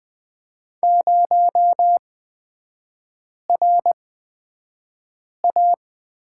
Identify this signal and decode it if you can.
Morse code, heard as a keyed tone, about 20 wpm, spaced as for 5 wpm, 700 Hz. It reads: 0RA
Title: 255 Lesson 41e - Period - Three Characters 20wpm